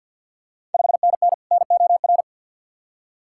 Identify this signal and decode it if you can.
Morse code, heard as a keyed tone, about 46 wpm, 690 Hz. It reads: 5NN NOR
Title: Ontario QSO Party Exchange 50wpm